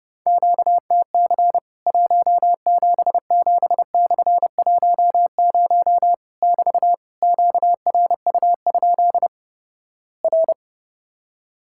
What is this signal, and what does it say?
Morse code, heard as a keyed tone, about 32 wpm, 700 Hz. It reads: QTC 177/10 <BT> QRU? R
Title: Worked All Europe DX Contest - QTC Exchange - 30wpm